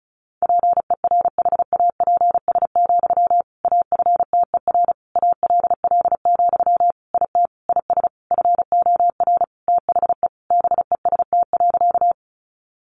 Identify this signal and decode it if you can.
Morse code, heard as a keyed tone, about 34 wpm, 700 Hz. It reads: PERHAPS, AFTER ALL, IT IS FOR THE BEST.
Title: Sentences from Top 500 Words 35wpm